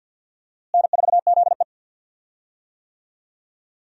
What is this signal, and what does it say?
Morse code, heard as a keyed tone, about 52 wpm, 700 Hz. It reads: N4ZE
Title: USA Call Signs - 4 Characters - Repeated 3x - 50wpm